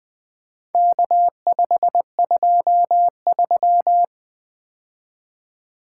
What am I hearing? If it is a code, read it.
Morse code, heard as a keyed tone, about 20 wpm, 700 Hz. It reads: K523
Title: Four Letter-Number Combinations - 20wpm